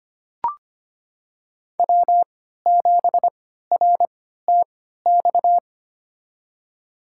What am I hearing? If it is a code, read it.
Morse code, heard as a keyed tone, about 25 wpm, spaced as for 14 wpm, 700 Hz. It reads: W7RTX
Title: USA Call Signs - 5 Characters - 3x Character Spacing - 25wpm